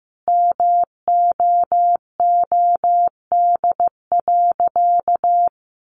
Morse code, heard as a keyed tone, about 15 wpm, 700 Hz. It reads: MOOD.
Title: Sentences from Top 1100 Words 15wpm